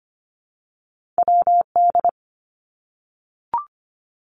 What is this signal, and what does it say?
Morse code, heard as a keyed tone, about 25 wpm, 700 Hz. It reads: WD
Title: Two Challenging Letter-Number Combinations 25wpm